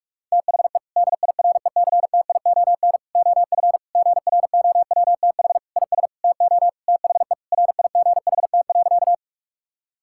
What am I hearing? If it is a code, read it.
Morse code, heard as a keyed tone, about 46 wpm, 700 Hz. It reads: THE DIRECTION OF GROWTH IS TO THE RIGHT.
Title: Sentences from Top 1000 Words - Review Part 5 - 45wpm